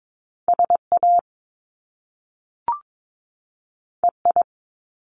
Morse code, heard as a keyed tone, about 22 wpm, 700 Hz. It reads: SA EI
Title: Top Two Letter Combinations in Words 22wpm